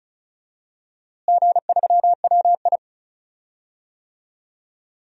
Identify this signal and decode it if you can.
Morse code, heard as a keyed tone, about 36 wpm, 700 Hz. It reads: G3WI
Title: DXCC-100 Call Signs - 4 Characters - Repeated 2x - 35wpm